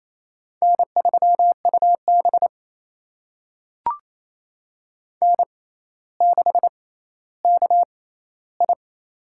Morse code, heard as a keyed tone, about 28 wpm, spaced as for 14 wpm, 700 Hz. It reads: N3UB N6KI
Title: USA Call Signs - 4 Characters - 6x Character Spacing - 28wpm